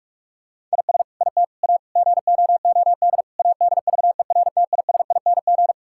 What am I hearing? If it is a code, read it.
Morse code, heard as a keyed tone, about 41 wpm, 700 Hz. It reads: IS IT A GOOD ADVERTISING
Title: Sentences from Top 1000 Words - Review Part 6 - 45wpm